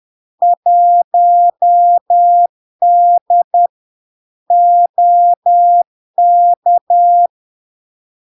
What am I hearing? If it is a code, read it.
Morse code, heard as a keyed tone, about 10 wpm, 700 Hz. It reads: 1D OK
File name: LARK-FD-2022-Log-in-CW-10wpm-700-Hz